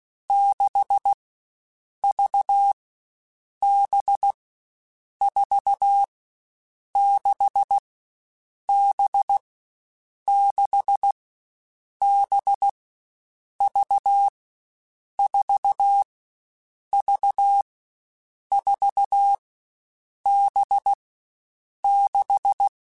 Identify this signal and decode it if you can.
Morse code, heard as a keyed tone, about 16 wpm, 800 Hz. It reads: 6 V B 4 6 B 6 B V 4 V 4 B 6
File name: Groups@13 / Groups@13 mp3s / VB46@13